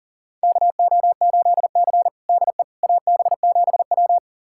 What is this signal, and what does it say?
Morse code, heard as a keyed tone, about 37 wpm, 700 Hz. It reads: KO8C DE AB7W
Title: USA Callsign DE Callsign - 4 Characters - 40wpm